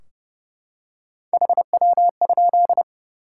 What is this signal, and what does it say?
Morse code, heard as a keyed tone, about 30 wpm, 700 Hz. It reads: HW?
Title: Top 180 QSO Elements 30wpm